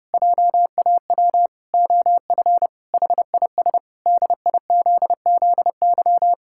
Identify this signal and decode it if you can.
Morse code, heard as a keyed tone, about 30 wpm, 700 Hz. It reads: JAW OF HIS DIZZY
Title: Pangram Challenge v2 - 30wpm